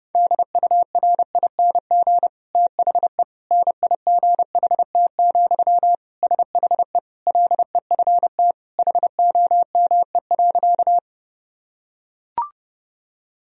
Morse code, heard as a keyed tone, about 30 wpm, 700 Hz. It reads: DURING THE NIGHT, SHE LEFT HOME.
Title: Sentences from Top 1000 Words - Review Part 2 - 30wpm